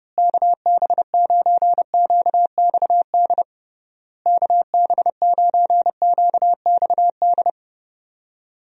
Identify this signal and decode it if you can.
Morse code, heard as a keyed tone, about 29 wpm, 700 Hz. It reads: KB9QXD KB9QXD
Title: USA Call Signs - 6 Characters - Repeated 3x - 30wpm